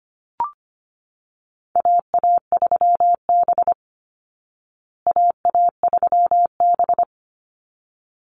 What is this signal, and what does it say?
Morse code, heard as a keyed tone, about 25 wpm, 700 Hz. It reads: AA3B AA3B
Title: Call Signs Repeated 3x - 25wpm